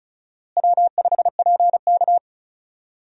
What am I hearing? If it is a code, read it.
Morse code, heard as a keyed tone, about 36 wpm, 700 Hz. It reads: W5PK
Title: USA Call Signs - 4 Characters - Repeated 3x - 35wpm